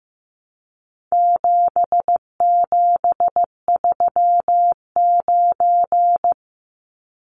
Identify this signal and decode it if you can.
Morse code, heard as a keyed tone, about 15 wpm, 700 Hz. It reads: 7739
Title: Numbers - 4-Digit 15wpm